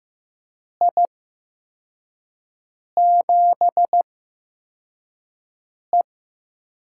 Morse code, heard as a keyed tone, about 15 wpm, spaced as for 4 wpm, 700 Hz. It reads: I7E
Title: USA Call Signs - 6 Characters - 8x Character Spacing - 15wpm